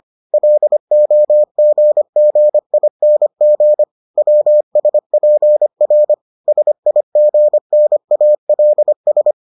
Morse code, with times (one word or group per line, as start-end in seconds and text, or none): 0.3-3.8 LOGGING
4.2-6.1 WSPR
6.5-9.3 SIGNALS